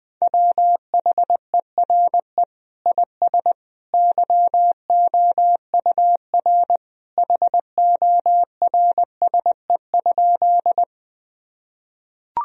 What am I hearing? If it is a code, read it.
Morse code, heard as a keyed tone, about 21 wpm, 700 Hz. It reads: WHERE IS YOUR HORSE?
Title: Sentences from Top 1000 Words - Review Part 1 - 20wpm